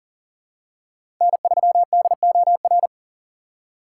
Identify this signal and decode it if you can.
Morse code, heard as a keyed tone, about 40 wpm, 700 Hz. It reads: N3DOR
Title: USA Call Signs - 5 Characters - 3x Character Spacing - 40wpm